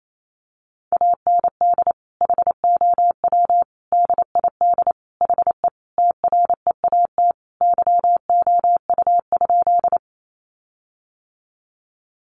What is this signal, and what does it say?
Morse code, heard as a keyed tone, about 28 wpm, 700 Hz. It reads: AND HOW DID HE TREAT YOU?
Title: Sentences from Top 1300 Words 28wpm